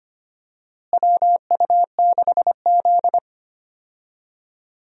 Morse code, heard as a keyed tone, about 25 wpm, 700 Hz. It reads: WU6Z
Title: USA Call Signs - 4 Characters - Repeated 2x - 25wpm